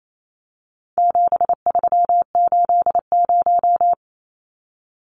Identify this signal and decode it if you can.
Morse code, heard as a keyed tone, about 28 wpm, 700 Hz. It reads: 7380
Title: Numbers - 4-Digit 28wpm